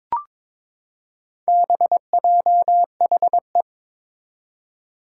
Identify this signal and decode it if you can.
Morse code, heard as a keyed tone, about 22 wpm, 700 Hz. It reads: BJHE
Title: Four Challenging Letter-Number Combinations 22wpm